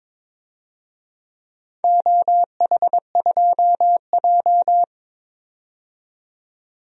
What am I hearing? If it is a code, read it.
Morse code, heard as a keyed tone, about 21 wpm, 700 Hz. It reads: OH2J
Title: DXCC-100 Call Signs - 4 Characters - 22wpm